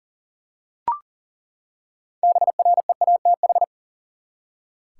Morse code, heard as a keyed tone, about 40 wpm, 700 Hz. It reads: BREATH
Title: Six Letter Words - 40wpm